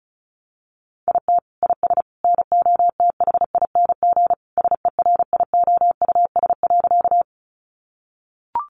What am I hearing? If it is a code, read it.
Morse code, heard as a keyed tone, about 34 wpm, 700 Hz. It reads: IT IS NOTHING SERIOUS.
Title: Sentences from Top 1100 Words 35wpm